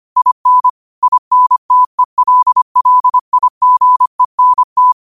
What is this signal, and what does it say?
Morse code, heard as a keyed tone, about 25 wpm, 1000 Hz. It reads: IN INTELLIGENT
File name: qotdfiles / 25wpm